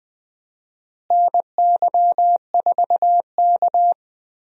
Morse code, heard as a keyed tone, about 20 wpm, 700 Hz. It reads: NY4K
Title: USA Call Signs - 4 Characters - Repeated 2x - 20wpm